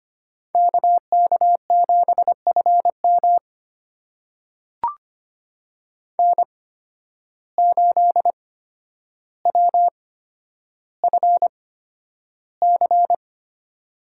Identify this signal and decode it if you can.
Morse code, heard as a keyed tone, about 25 wpm, spaced as for 11 wpm, 700 Hz. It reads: KK7FM N8WFC
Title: USA Call Signs - 5 Characters - 8x Character Spacing - 25wpm